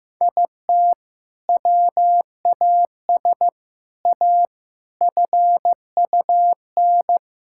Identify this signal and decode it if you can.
Morse code, heard as a keyed tone, about 15 wpm, 700 Hz. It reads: IT WAS A FUN
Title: Sentences from Top 1000 Words - Review Part 5 - 15wpm